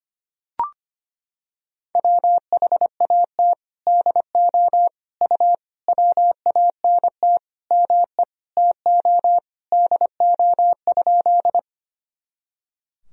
Morse code, heard as a keyed tone, about 25 wpm, 700 Hz. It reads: WHAT DO U WANT ME TO DO?
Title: Sentences from Top 200 Words and Abbreviations 25wpm